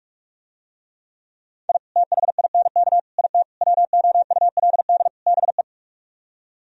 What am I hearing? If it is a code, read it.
Morse code, heard as a keyed tone, about 46 wpm, 700 Hz. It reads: I THINK IT WOULD BE